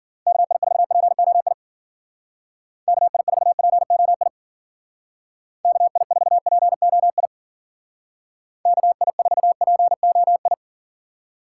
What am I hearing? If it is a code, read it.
Morse code, heard as a keyed tone, about 49 wpm, 700 Hz. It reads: KI4POI KI4POI KI4POI KI4POI
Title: Speed Racing - USA Call Signs 40wpm